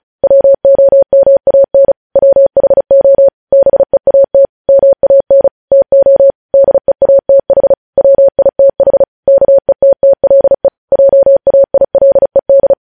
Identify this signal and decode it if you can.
Morse code, heard as a keyed tone, about 36 wpm, 550 Hz. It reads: WOMAN WHO BEAT MAN TO DEATH WITH KETTLE JAILED
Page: News headlines in Morse code at 35 wpm